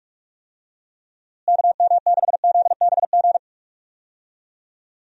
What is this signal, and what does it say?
Morse code, heard as a keyed tone, about 41 wpm, 700 Hz. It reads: KM6ZBG
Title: USA Call Signs - 6 Characters - Repeated 2x - 45wpm